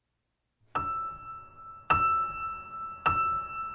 There is no code.